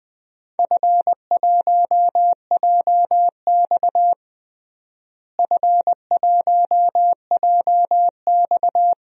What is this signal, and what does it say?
Morse code, heard as a keyed tone, about 20 wpm, 700 Hz. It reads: F1JX F1JX
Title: DXCC-100 Call Signs - 4 Characters - Repeated 2x - 20wpm